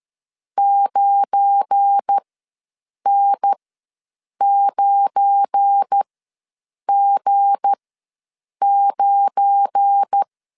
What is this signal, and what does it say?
Morse code, heard as a keyed tone, about 13 wpm, spaced as for 7 wpm, 800 Hz. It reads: 9N9G9
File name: LEZIONI / 7 GF9Z / 7 GF9 ws 12wpm 3344